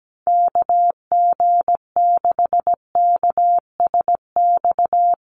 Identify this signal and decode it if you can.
Morse code, heard as a keyed tone, about 17 wpm, 700 Hz. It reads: KG6KSX
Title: USA Call Signs Repeated 3x 17wpm